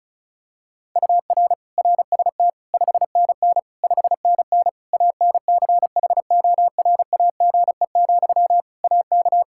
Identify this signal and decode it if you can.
Morse code, heard as a keyed tone, about 36 wpm, 700 Hz. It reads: UR RST 5NN 5NN ANCHORAGE, AK
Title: Practice QSO - Exchange 1 - Beginning - 35wpm